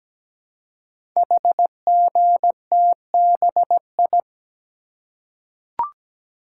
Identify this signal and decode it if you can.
Morse code, heard as a keyed tone, about 17 wpm, 700 Hz. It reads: HGTBI
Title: Code Groups - 5 Letters - 17wpm